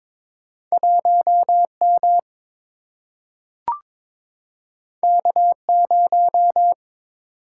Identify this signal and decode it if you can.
Morse code, heard as a keyed tone, about 22 wpm, 700 Hz. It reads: 1 M K 0
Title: Single-Letter Single-Number 22wpm